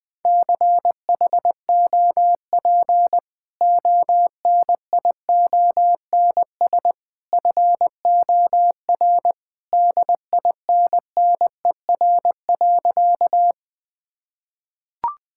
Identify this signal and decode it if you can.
Morse code, heard as a keyed tone, about 19 wpm, 700 Hz. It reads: CHOP ONIONS FOR DINNER.
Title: Easy Sentences with Hints - 20wpm